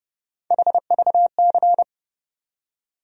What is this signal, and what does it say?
Morse code, heard as a keyed tone, about 30 wpm, 700 Hz. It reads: HVC
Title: USA Call Sign Postfixes - 3 Letters 30wpm